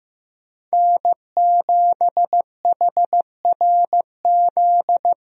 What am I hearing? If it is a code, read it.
Morse code, heard as a keyed tone, about 15 wpm, 700 Hz. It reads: N7HRZ
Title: USA Call Signs - 5 Characters - 15wpm